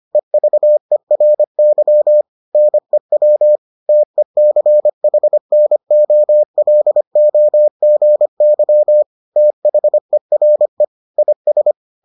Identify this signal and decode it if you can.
Morse code, heard as a keyed tone, about 25 wpm, 600 Hz. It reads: EVERY NEW TECHNOLOGY THERE IS